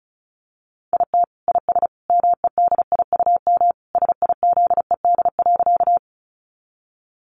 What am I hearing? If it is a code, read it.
Morse code, heard as a keyed tone, about 34 wpm, 700 Hz. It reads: IT IS MEDIUM SIZED.
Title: Easy Phrases 35wpm